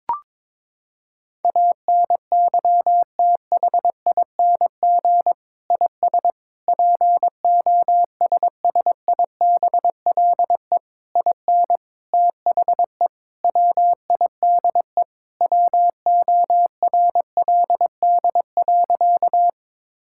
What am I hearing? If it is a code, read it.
Morse code, heard as a keyed tone, about 22 wpm, 700 Hz. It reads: ANYTHING IS POSSIBLE IN THE WIDE WORLD.
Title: Sentences from Top 1000 Words - Review Part 6 - 22wpm